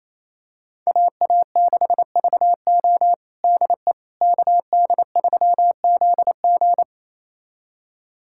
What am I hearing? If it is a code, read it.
Morse code, heard as a keyed tone, about 28 wpm, 700 Hz. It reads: AA6VO DE KD3ZG
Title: USA Callsign DE Callsign - 5 Characters - 28wpm